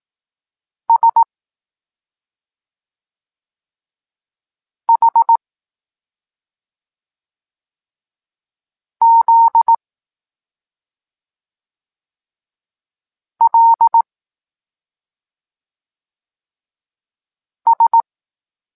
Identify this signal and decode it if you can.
Morse code, heard as a keyed tone, about 18 wpm, 920 Hz. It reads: S H Z L S
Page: Tekst 23 – Doskonalenie odbioru poznanych liter